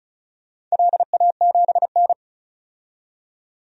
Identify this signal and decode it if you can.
Morse code, heard as a keyed tone, about 35 wpm, 700 Hz. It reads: LA7N
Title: DXCC-100 Call Signs - 4 Characters - 35wpm